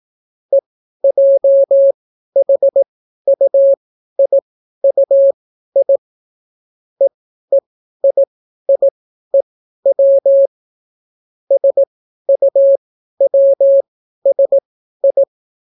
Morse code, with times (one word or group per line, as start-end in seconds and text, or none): 0.5-6.0 EJHUIUI
7.0-10.5 EEIIEW
11.5-15.2 SUWSI